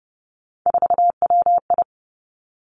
Morse code, heard as a keyed tone, about 30 wpm, 700 Hz. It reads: 4WI
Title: Three Challenging Letter-Number Combinations 30wpm